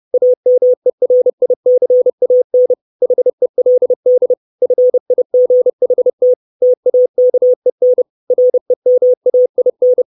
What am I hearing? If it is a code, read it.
Morse code, heard as a keyed tone, about 30 wpm, 500 Hz. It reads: AMERICAN HELD FIGHT TAKEN REMAIN
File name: 632 Words 30WPM